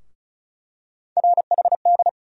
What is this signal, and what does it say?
Morse code, heard as a keyed tone, about 36 wpm, 700 Hz. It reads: RHD
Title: USA Call Sign Postfixes - 3 Letters 35wpm